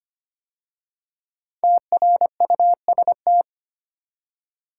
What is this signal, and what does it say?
Morse code, heard as a keyed tone, about 25 wpm, 700 Hz. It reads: TRUST
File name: 5 Letter Words - First 3 Letters Given - 25wpm